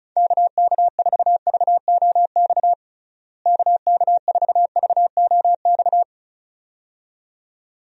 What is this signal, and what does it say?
Morse code, heard as a keyed tone, about 36 wpm, 700 Hz. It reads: KK4VOX KK4VOX
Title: USA Call Signs - 6 Characters - Repeated 2x - 35wpm